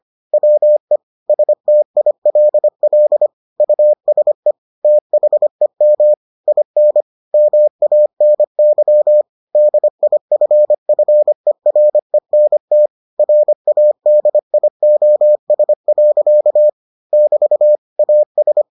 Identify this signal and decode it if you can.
Morse code, heard as a keyed tone, about 25 wpm, 600 Hz. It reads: WE STILL USE THEM IN MANY DIFFERENT RADIOS. <BT> AS